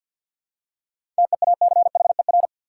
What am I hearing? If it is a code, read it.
Morse code, heard as a keyed tone, about 53 wpm, 700 Hz. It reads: TEACHER
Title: Seven Letter Words - 50wpm